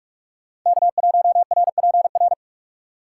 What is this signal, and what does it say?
Morse code, heard as a keyed tone, about 40 wpm, 700 Hz. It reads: K1RPR